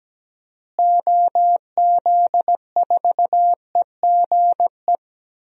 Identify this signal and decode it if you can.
Morse code, heard as a keyed tone, about 17 wpm, 700 Hz. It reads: OZ4EGE
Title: DXCC-100 Call Signs - 6 Characters - Repeated 3x - 17wpm